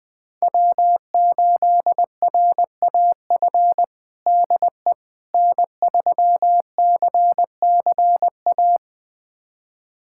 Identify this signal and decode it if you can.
Morse code, heard as a keyed tone, about 21 wpm, 700 Hz. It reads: W8RAF DE N3CCA
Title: USA Callsign DE Callsign - 5 Characters - 20wpm